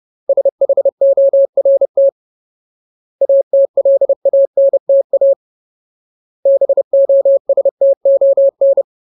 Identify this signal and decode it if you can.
Morse code, heard as a keyed tone, about 30 wpm, 550 Hz. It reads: SHORT ATLANTA BOSTON
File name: Ultimate Word List / 30WPM Ultimate